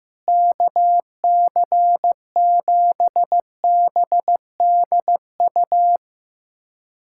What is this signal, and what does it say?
Morse code, heard as a keyed tone, about 15 wpm, 700 Hz. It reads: KC7BDU
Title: USA Callsign DE Callsign - 6 Characters - 15wpm